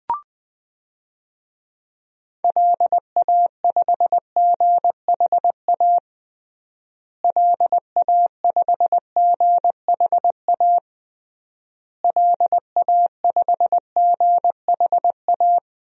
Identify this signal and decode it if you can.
Morse code, heard as a keyed tone, about 20 wpm, 700 Hz. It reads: LA5GHA LA5GHA LA5GHA
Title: DXCC-100 Call Signs - 6 Characters - Repeated 3x - 20wpm